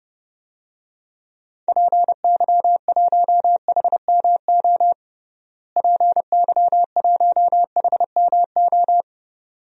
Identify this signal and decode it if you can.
Morse code, heard as a keyed tone, about 30 wpm, 700 Hz. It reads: PY1HMO PY1HMO
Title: DXCC-100 Call Signs - 6 Characters - Repeated 2x - 30wpm